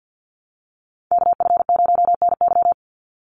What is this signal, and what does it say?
Morse code, heard as a keyed tone, about 50 wpm, 700 Hz. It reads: KF0NY
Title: USA Call Signs 50wpm